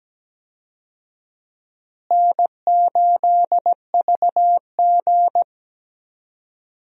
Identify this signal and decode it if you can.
Morse code, heard as a keyed tone, about 17 wpm, 700 Hz. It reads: N8VG
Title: USA Call Signs - 4 Characters - Repeated 2x - 17wpm